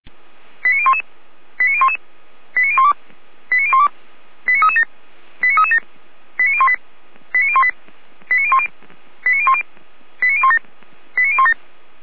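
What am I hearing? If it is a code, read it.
Five-tone selective call (ZVEI), heard as quick runs of five tones, tones of about 70 ms. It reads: 89011 89011 89012 89012 89337 89337 89018 89018 89010 89010 89017 89017
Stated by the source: ZVEI five-tone selective call received on an SDR.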